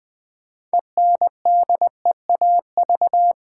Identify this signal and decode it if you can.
Morse code, heard as a keyed tone, about 20 wpm, 700 Hz. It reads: ENDEAV
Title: Build A Word - 3 Letters At A Time - 8-Letter Words - 20wpm